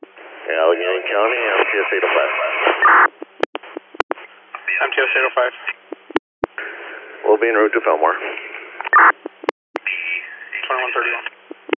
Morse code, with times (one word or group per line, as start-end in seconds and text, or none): none